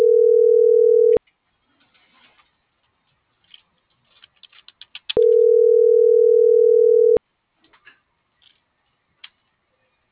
Ringback tone (North America), 440 + 480 Hz.